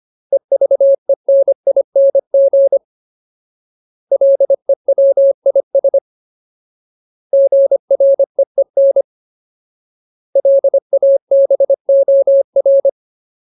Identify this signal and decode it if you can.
Morse code, heard as a keyed tone, about 25 wpm, 550 Hz. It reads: EVENING LEWIS GREEN LABOR
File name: Ultimate Word List / 25WPM Ultimate